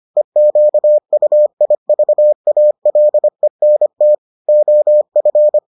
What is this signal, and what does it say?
Morse code, heard as a keyed tone, about 25 wpm, 600 Hz. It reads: EQUIVALENT OF